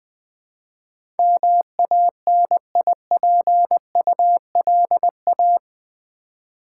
Morse code, heard as a keyed tone, about 20 wpm, 700 Hz. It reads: MANIPULA
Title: Build A Word - 4 Letters At A Time - 12-Letter Words - 20wpm